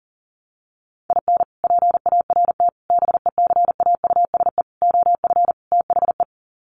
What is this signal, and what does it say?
Morse code, heard as a keyed tone, about 40 wpm, 700 Hz. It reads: IN PART BECAUSE OF THE
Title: Sets of 5 Words Encore2 40wpm